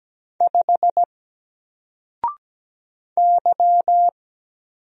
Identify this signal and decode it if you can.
Morse code, heard as a keyed tone, about 17 wpm, 700 Hz. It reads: 5 Y
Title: Single Letter-Number 17wpm